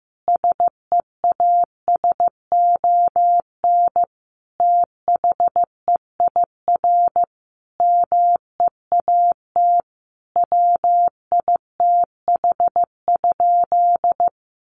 Morse code, heard as a keyed tone, about 14 wpm, 700 Hz. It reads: SEASON THEIR MEAT WITH?
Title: Sentences from Top 1400 Words 15wpm